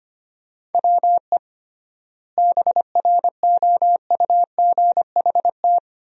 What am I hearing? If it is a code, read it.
Morse code, heard as a keyed tone, about 25 wpm, 700 Hz. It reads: WE BROUGHT
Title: 3x Word-Spacing - Sentences from Top 500 Words - 25wpm